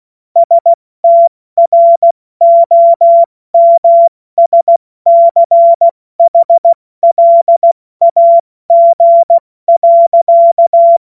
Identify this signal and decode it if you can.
Morse code, heard as a keyed tone, about 16 wpm, 680 Hz. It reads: STROMSCHLAG.
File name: Morsekurs / Witze / Witz 17 Juni Mai 2025 70 F80